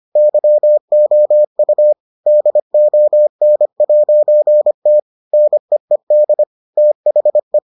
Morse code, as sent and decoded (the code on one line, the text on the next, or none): -.-- --- ..- / -.. --- -. .----. - / -. . -.. / - .... .
YOU DON'T NEED THE